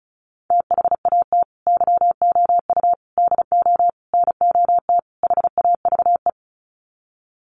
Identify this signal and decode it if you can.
Morse code, heard as a keyed tone, about 34 wpm, 700 Hz. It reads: THAT YOU DO NOT HAVE
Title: Sets of 5 Words Encore 35wpm